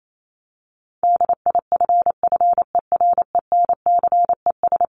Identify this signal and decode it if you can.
Morse code, heard as a keyed tone, about 28 wpm, 700 Hz. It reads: DIFFERENCES
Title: ES:S Suffix - Top 100 Words with Base Words 28wpm